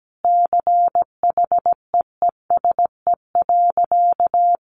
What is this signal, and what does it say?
Morse code, heard as a keyed tone, about 17 wpm, 700 Hz. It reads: CHEESE.
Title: Easy Phrases 17wpm